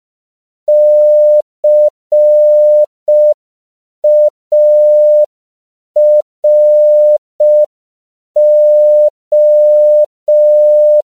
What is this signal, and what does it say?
Morse code, heard as a keyed tone, about 5 wpm, 600 Hz. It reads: CARO
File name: CW / 5PPM